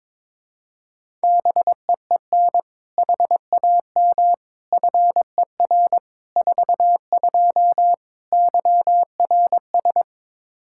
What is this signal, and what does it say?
Morse code, heard as a keyed tone, about 22 wpm, 700 Hz. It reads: BEEN HAM FER 42 YRS
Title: Practice QSO - Exchange 3 - More Info - 22wpm